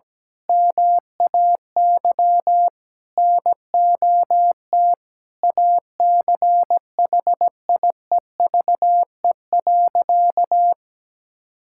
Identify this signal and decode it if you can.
Morse code, heard as a keyed tone, about 17 wpm, 700 Hz. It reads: MAY NOT ACHIEVE.
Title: Sentences from Top 1000 Words - Review Part 6 - 17wpm